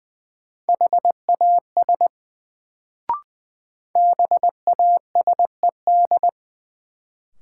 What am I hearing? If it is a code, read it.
Morse code, heard as a keyed tone, about 20 wpm, 700 Hz. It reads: HAS BASED
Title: Sets of 2 Words - New Word Drill - 20wpm